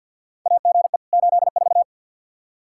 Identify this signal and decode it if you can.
Morse code, heard as a keyed tone, about 53 wpm, 700 Hz. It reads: AGE 84
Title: Practice QSO - Exchange 3 - More Info - 50wpm